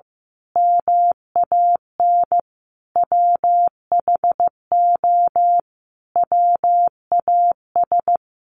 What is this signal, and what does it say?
Morse code, heard as a keyed tone, about 15 wpm, 700 Hz. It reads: MAN WHO WAS
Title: A Sound Of Thunder 15wpm